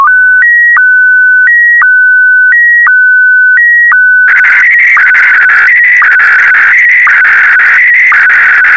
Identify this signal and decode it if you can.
SSTV (slow-scan television), Scottie DX.